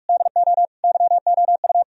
Morse code, heard as a keyed tone, about 64 wpm, spaced as for 55 wpm, 700 Hz. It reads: DO YOU
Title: Speed-Racing - Sets of 4 Words 45wpm